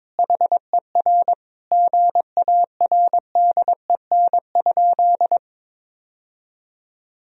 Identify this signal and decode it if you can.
Morse code, heard as a keyed tone, about 22 wpm, 700 Hz. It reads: HER GARDEN?
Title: Sentences from Top 1000 Words - Review Part 4 - 22wpm